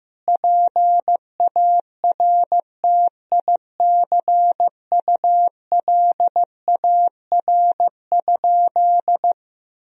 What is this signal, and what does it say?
Morse code, heard as a keyed tone, about 15 wpm, 700 Hz. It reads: PARTICULAR?
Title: Sentences from Top 1000 Words - Review Part 4 - 15wpm